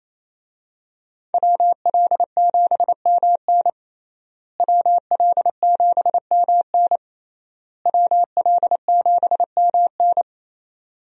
Morse code, heard as a keyed tone, about 28 wpm, 700 Hz. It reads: WL7MN WL7MN WL7MN
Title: USA Call Signs - 5 Characters - Repeated 3x - 28wpm